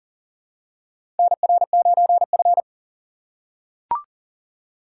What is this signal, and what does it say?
Morse code, heard as a keyed tone, about 38 wpm, 700 Hz. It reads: NR9F
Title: USA Call Signs - 2x Character Spacing - 40wpm